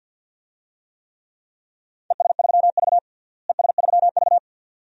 Morse code, heard as a keyed tone, about 48 wpm, 700 Hz. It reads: ES3V ES3V
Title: DXCC-100 Call Signs - 4 Characters - Repeated 2x - 50wpm